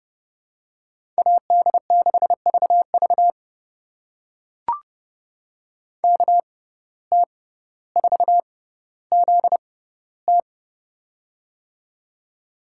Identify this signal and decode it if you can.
Morse code, heard as a keyed tone, about 31 wpm, spaced as for 15 wpm, 700 Hz. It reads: AD6VV KT4ZT
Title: USA Call Signs - 5 Characters - 6x Character Spacing - 30wpm